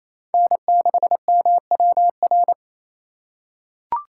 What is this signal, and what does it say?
Morse code, heard as a keyed tone, about 28 wpm, 700 Hz. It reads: N6MWR
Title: USA Call Signs - 5 Characters - 2x Character Spacing - 28wpm